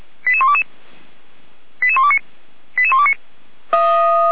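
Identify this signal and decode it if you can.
Five-tone selective call (ZVEI), heard as quick runs of five tones, tones of about 70 ms. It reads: 88122 88129 88129
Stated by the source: ZVEI five-tone selective call received by SDR.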